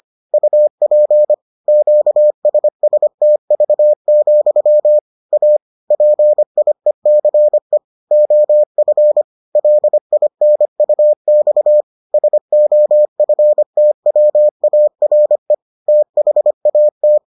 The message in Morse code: UP QSSTV, A PIECE OF LINUX SOFTWARE THAT

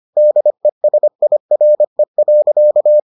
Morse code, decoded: DESIRE.

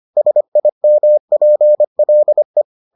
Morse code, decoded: SIMPLE

